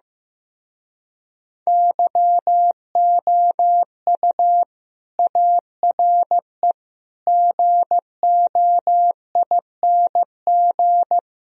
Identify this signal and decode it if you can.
Morse code, heard as a keyed tone, about 15 wpm, 700 Hz. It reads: YOU ARE GOING
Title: Speed-Racing - Sets of 4 Words 15wpm